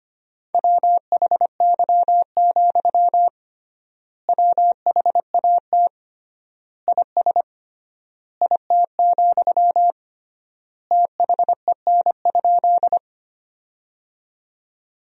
Morse code, heard as a keyed tone, about 25 wpm, 700 Hz. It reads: WHY, WHAT IS IT, THEN?
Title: 3x Word-Spacing - Sentences from Top 200 Words - 25wpm